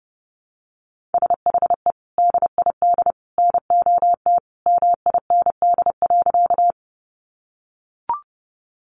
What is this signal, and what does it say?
Morse code, heard as a keyed tone, about 30 wpm, 700 Hz. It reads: SHE DID NOT MIND.
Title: Sentences from Top 600 Words 30wpm